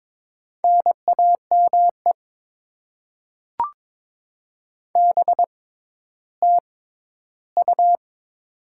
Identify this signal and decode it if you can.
Morse code, heard as a keyed tone, about 22 wpm, 690 Hz. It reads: N A M E B T U
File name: Top 20 QSO Elements - 6x Character Spacing - 22wpm